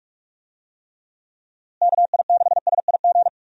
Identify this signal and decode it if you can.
Morse code, heard as a keyed tone, about 48 wpm, 700 Hz. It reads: KI6SIG
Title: USA Call Signs - 6 Characters - Repeated 3x - 45wpm